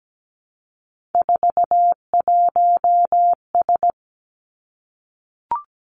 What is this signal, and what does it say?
Morse code, heard as a keyed tone, about 17 wpm, 700 Hz. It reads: 41S
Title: Three Challenging Letter-Number Combinations 17wpm